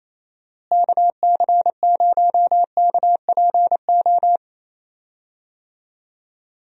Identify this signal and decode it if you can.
Morse code, heard as a keyed tone, about 28 wpm, 700 Hz. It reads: KC0KPO
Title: USA Call Signs - 6 Characters - Repeated 3x - 28wpm